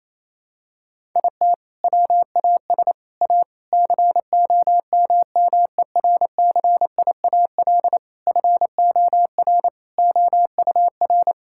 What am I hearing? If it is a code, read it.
Morse code, heard as a keyed tone, about 28 wpm, 700 Hz. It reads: IT WAS A COMMERCIAL FOR OUR
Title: Sentences from Top 1000 Words - Review Part 5 - 28wpm